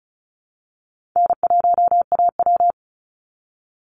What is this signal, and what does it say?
Morse code, heard as a keyed tone, about 34 wpm, 700 Hz. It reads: N1AW
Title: NA SOTA Chaser Callsigns 35wpm